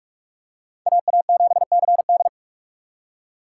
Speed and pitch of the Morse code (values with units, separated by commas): 41 wpm, 700 Hz